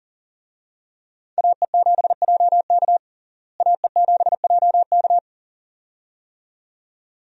Morse code, heard as a keyed tone, about 40 wpm, 700 Hz. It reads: AE7JK AE7JK
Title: USA Call Signs - 5 Characters - Repeated 2x - 40wpm